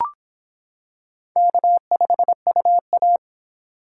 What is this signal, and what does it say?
Morse code, heard as a keyed tone, about 26 wpm, 700 Hz. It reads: K5UA